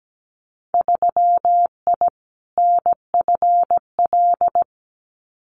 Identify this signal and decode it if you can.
Morse code, heard as a keyed tone, about 17 wpm, 700 Hz. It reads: 3I NFL
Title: Winter Field Day Exchanges 17wpm